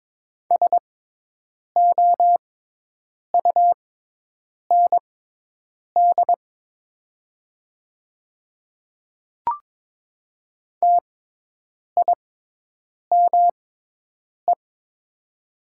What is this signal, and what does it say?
Morse code, heard as a keyed tone, about 22 wpm, spaced as for 8 wpm, 700 Hz. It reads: SOUND TIME
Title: 6x Character Spacing - Top 100 Words - 22wpm